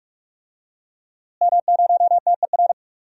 Morse code, heard as a keyed tone, about 47 wpm, 700 Hz. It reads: M0TER
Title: DXCC-100 Call Signs - 5 Characters - Repeated 2x - 45wpm